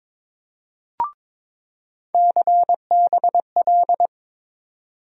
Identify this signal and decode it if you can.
Morse code, heard as a keyed tone, about 22 wpm, 700 Hz. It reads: CBL
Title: USA Call Sign Postfixes - 3 Letters 22wpm